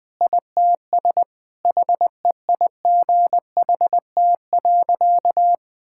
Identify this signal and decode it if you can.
Morse code, heard as a keyed tone, about 20 wpm, 700 Hz. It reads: ITS HEIGHT.
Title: Sentences from Top 1600 Words 20wpm